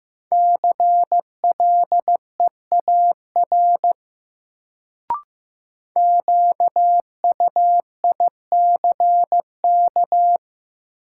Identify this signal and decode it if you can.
Morse code, heard as a keyed tone, about 16 wpm, 700 Hz. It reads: CLEAR QUICK
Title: Sentences from Top 500 Words - New Word Drill - 15wpm